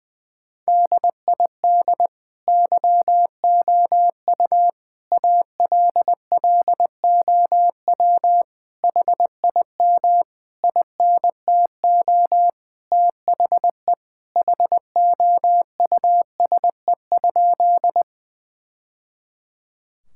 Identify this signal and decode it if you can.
Morse code, heard as a keyed tone, about 20 wpm, 700 Hz. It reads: DID YOU ALLOW HIM INTO THE HOUSE?
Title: Sentences from Top 1000 Words - Review Part 5 - 20wpm